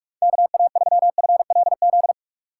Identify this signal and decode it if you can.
Morse code, heard as a keyed tone, about 44 wpm, 700 Hz. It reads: KA3FLZ
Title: USA Call Signs - 6 Characters - 6x Character Spacing - 45wpm